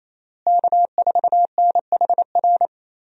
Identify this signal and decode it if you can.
Morse code, heard as a keyed tone, about 28 wpm, 700 Hz. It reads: K4NHR
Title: USA Call Signs - 5 Characters - Repeated 3x - 28wpm